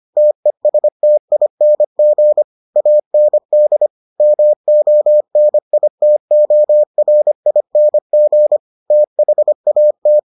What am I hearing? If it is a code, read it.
Morse code, heard as a keyed tone, about 25 wpm, 600 Hz. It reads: TESTING AND MONITORING THAT